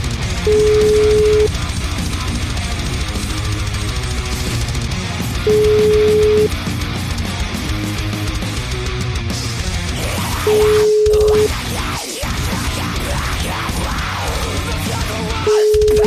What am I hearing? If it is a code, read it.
Ringback tone (Europe, 425 Hz), with noise.